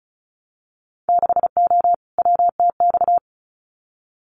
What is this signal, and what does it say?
Morse code, heard as a keyed tone, about 34 wpm, 700 Hz. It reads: 6O WTX
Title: Winter Field Day Exchanges Encore 35wpm